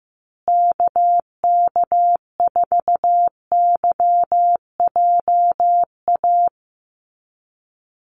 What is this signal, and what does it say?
Morse code, heard as a keyed tone, about 15 wpm, 700 Hz. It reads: KK4YJA
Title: USA Call Signs Repeated 2x 15wpm